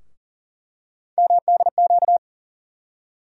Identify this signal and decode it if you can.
Morse code, heard as a keyed tone, about 40 wpm, 700 Hz. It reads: MDQ
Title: USA Call Sign Postfixes - 3 Letters 40wpm